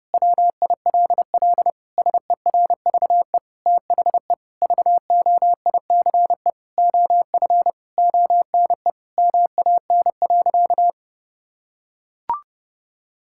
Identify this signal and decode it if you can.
Morse code, heard as a keyed tone, about 30 wpm, 700 Hz. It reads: WILL SERVE THE VOICE OF ONE MAN.
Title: Sentences from Top 1000 Words - Review Part 6 - 30wpm